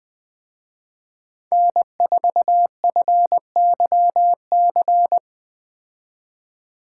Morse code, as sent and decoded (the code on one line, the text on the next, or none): -. ....- ..-. -.-- -.-.
N4FYC